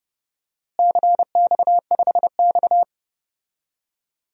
Morse code, heard as a keyed tone, about 30 wpm, 700 Hz. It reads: CX5X
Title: DXCC-100 Call Signs - 4 Characters - Repeated 2x - 30wpm